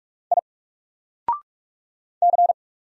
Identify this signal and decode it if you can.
Morse code, heard as a keyed tone, about 46 wpm, 700 Hz. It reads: I C
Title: Single Letter-Number Alt 45wpm